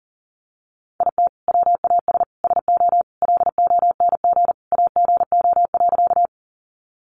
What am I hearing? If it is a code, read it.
Morse code, heard as a keyed tone, about 40 wpm, 700 Hz. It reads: IT WAS SO LONG AGO.
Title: Sentences from Top 500 Words 40wpm